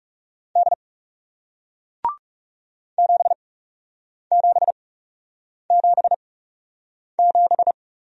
Morse code, heard as a keyed tone, about 38 wpm, spaced as for 8 wpm, 700 Hz. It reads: D 7777